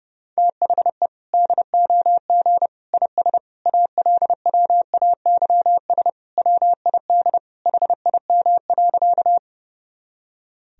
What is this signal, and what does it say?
Morse code, heard as a keyed tone, about 30 wpm, 700 Hz. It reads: THE DOG IS ALWAYS WID HIM.
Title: Sentences from Top 400 Words and Abbreviations 30wpm